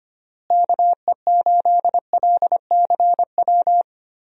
Morse code, heard as a keyed tone, about 25 wpm, 700 Hz. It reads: KE8LCW
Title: USA Call Signs - 6 Characters - Repeated 2x - 25wpm